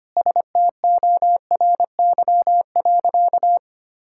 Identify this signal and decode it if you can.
Morse code, heard as a keyed tone, about 25 wpm, 690 Hz. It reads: STORY.
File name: Easy Sentences with Hints - 25wpm